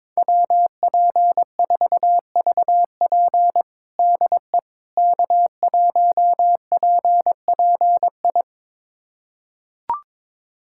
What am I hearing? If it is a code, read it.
Morse code, heard as a keyed tone, about 22 wpm, 700 Hz. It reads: WP4VP DE K1PPI
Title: USA Callsign DE Callsign - 5 Characters - 22wpm